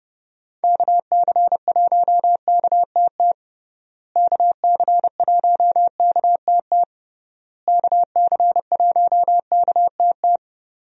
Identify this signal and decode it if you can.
Morse code, heard as a keyed tone, about 30 wpm, 700 Hz. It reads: KC1KTT KC1KTT KC1KTT
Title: USA Call Signs - 6 Characters - Repeated 3x - 30wpm